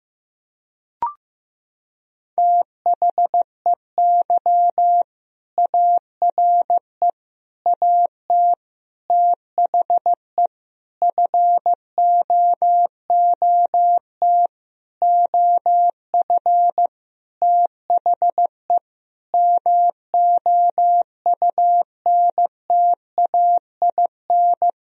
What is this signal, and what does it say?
Morse code, heard as a keyed tone, about 15 wpm, 700 Hz. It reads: THEY ARE AT THE FOOT OF THE MOUNTAIN
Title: Sentences from Top 1000 Words - Review Part 4 - 15wpm